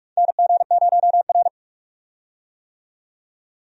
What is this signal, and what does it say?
Morse code, heard as a keyed tone, about 46 wpm, 700 Hz. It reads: NG0R
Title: USA Call Signs - 4 Characters - 45wpm